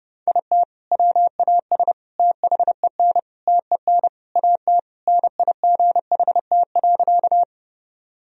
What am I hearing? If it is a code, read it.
Morse code, heard as a keyed tone, about 30 wpm, 700 Hz. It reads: IT WAS THEN TEN AT NIGHT.
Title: Sentences from Top 1000 Words - Review Part 2 - 30wpm